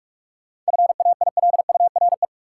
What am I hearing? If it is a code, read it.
Morse code, heard as a keyed tone, about 41 wpm, 700 Hz. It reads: FAILURE
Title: Sentences from Top 900 Words - New Word Drill - 45wpm